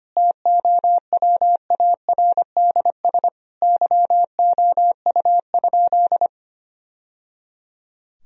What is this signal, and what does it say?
Morse code, heard as a keyed tone, about 25 wpm, 700 Hz. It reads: TOWARDS YOU?
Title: Sentences from Top 1000 Words - Review Part 2 - 25wpm